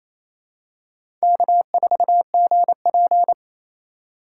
Morse code, heard as a keyed tone, about 28 wpm, 700 Hz. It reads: K4GP
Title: USA Call Signs - 4 Characters - 3x Character Spacing - 28wpm